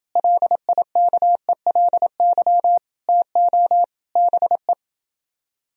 Morse code, heard as a keyed tone, about 27 wpm, 700 Hz. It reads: LIKELY TO BE